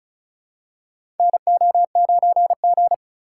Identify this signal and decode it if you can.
Morse code, heard as a keyed tone, about 36 wpm, 700 Hz. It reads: NO9G
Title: USA Call Signs - 4x Character Spacing - 35wpm